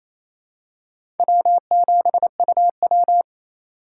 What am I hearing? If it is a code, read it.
Morse code, heard as a keyed tone, about 28 wpm, 700 Hz. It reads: W7UW
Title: USA Call Signs - 4 Characters - 2x Character Spacing - 28wpm